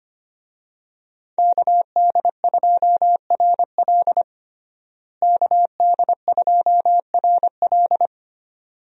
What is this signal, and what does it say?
Morse code, heard as a keyed tone, about 25 wpm, 700 Hz. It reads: KD2RL KD2RL
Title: USA Call Signs - 5 Characters - Repeated 2x - 25wpm